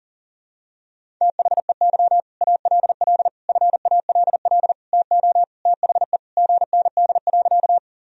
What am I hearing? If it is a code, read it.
Morse code, heard as a keyed tone, about 40 wpm, 700 Hz. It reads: THEY ALL FALL TO THE GND.
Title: Sentences from Top 500 Words and Abbreviations 40wpm